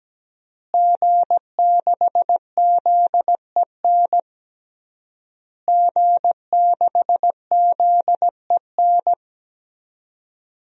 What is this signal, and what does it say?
Morse code, heard as a keyed tone, about 17 wpm, 700 Hz. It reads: G6ZEN G6ZEN
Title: DXCC-100 Call Signs - 5 Characters - Repeated 3x - 17wpm